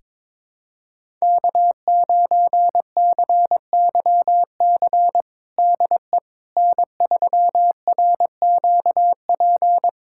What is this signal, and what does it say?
Morse code, heard as a keyed tone, about 22 wpm, 700 Hz. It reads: K9CYC DE N3RQP
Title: USA Callsign DE Callsign - 5 Characters - 22wpm